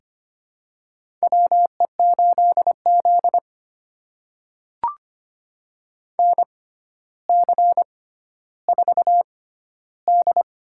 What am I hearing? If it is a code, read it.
Morse code, heard as a keyed tone, about 25 wpm, spaced as for 13 wpm, 700 Hz. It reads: WE8Z NC4D
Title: USA Call Signs - 6x Character Spacing - 25wpm